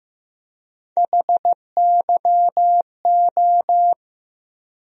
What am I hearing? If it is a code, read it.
Morse code, heard as a keyed tone, about 15 wpm, 700 Hz. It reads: HYO
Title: USA Call Sign Postfixes - 3 Letters 15wpm